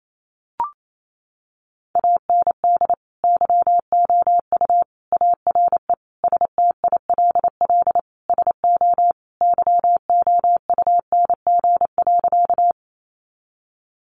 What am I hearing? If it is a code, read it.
Morse code, heard as a keyed tone, about 28 wpm, 700 Hz. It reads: AND YOU ARE STILL SO YOUNG.
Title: Sentences from Top 400 Words 28wpm